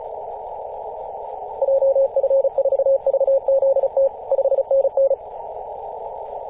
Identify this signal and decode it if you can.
Morse code, heard as a keyed tone, about 34 wpm, 540 Hz. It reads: JF4VZT 5NN
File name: 2016/Apr/09 0838z HV0A 18.071MHz CW